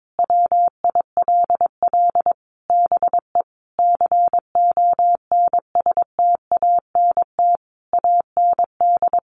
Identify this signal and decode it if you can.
Morse code, heard as a keyed tone, about 22 wpm, 700 Hz. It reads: WILL BE CONSTANT AND
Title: Sentences from Top 1500 Words 22wpm